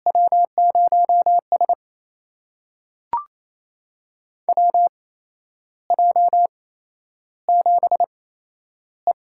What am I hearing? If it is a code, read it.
Morse code, heard as a keyed tone, about 28 wpm, spaced as for 11 wpm, 700 Hz. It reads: W0S WJ7E
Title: USA Call Signs - 8x Character Spacing - 28wpm